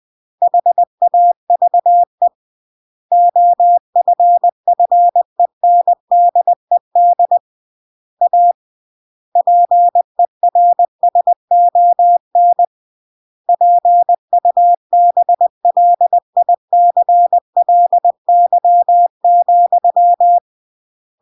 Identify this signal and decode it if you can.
Morse code, heard as a keyed tone, about 20 wpm, 700 Hz. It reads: HAVE OFFENDED A PERSON PUBLICLY,
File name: CW / jodorowsky / 82 commandments 20-20